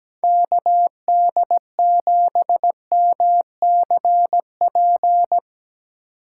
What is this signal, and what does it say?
Morse code, heard as a keyed tone, about 18 wpm, 700 Hz. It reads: KD7MCP
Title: Speed Racing - USA 2x3 Call Signs 17wpm